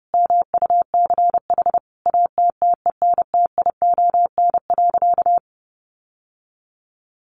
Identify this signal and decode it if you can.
Morse code, heard as a keyed tone, about 30 wpm, 700 Hz. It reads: MUCH ATTENTION.
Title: Sentences from Top 1000 Words 30wpm